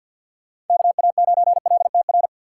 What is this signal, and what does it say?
Morse code, heard as a keyed tone, about 53 wpm, 700 Hz. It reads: KA9LTR
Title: USA Call Signs - 4x Character Spacing - 50wpm